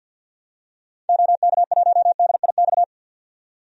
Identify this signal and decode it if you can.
Morse code, heard as a keyed tone, about 46 wpm, 700 Hz. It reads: OK1DIX
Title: DXCC-100 Call Signs - 6 Characters - 50wpm